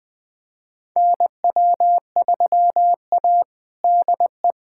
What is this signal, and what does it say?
Morse code, heard as a keyed tone, about 20 wpm, 700 Hz. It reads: NW3A DE
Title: USA Callsign DE Callsign - 4 Characters - 20wpm